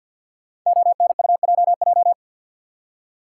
Morse code, heard as a keyed tone, about 49 wpm, 700 Hz. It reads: ONUJJ
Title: Code Groups - 5 Letters - 50wpm